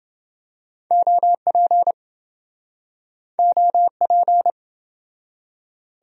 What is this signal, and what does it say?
Morse code, heard as a keyed tone, about 28 wpm, 700 Hz. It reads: OP OP